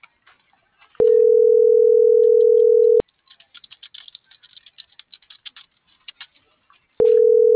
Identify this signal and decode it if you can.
Ringback tone (North America), 440 + 480 Hz.